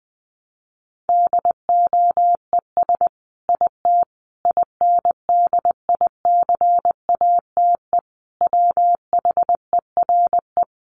Morse code, heard as a keyed tone, about 20 wpm, 700 Hz. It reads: DOES IT INDICATE WHERE
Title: Sentences from Top 1300 Words 20wpm